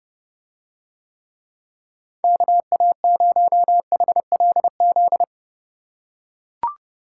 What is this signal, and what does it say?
Morse code, heard as a keyed tone, about 30 wpm, 700 Hz. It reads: KA0HLZ